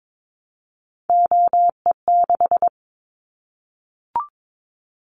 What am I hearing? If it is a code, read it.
Morse code, heard as a keyed tone, about 22 wpm, 700 Hz. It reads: OE6
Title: Callsign Prefixes for Top 25 DXCC - 22wpm